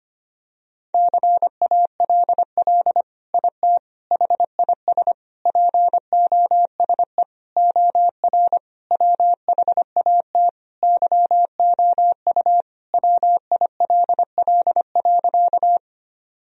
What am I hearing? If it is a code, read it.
Morse code, heard as a keyed tone, about 25 wpm, 700 Hz. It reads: CALL IT HIS POSE OR WHAT YOU WILL.
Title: Sentences from Top 1000 Words - Review Part 2 - 25wpm